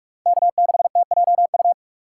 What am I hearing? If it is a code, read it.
Morse code, heard as a keyed tone, about 45 wpm, 700 Hz. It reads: K6TJU